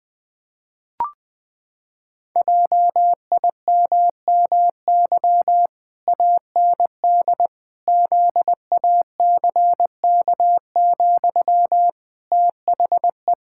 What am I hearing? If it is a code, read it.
Morse code, heard as a keyed tone, about 21 wpm, 700 Hz. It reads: JIMMY AND ZACK, THE
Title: Pangram Challenge v2 - 20wpm